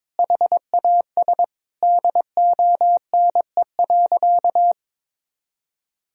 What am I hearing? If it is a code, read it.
Morse code, heard as a keyed tone, about 22 wpm, 700 Hz. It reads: HAS DONE.
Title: Sentences from Top 1000 Words - Review Part 1 - 22wpm